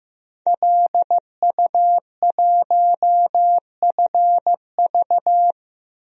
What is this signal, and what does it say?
Morse code, heard as a keyed tone, about 16 wpm, 700 Hz. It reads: LU1FV
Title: DXCC-100 Call Signs - 5 Characters - Repeated 3x - 15wpm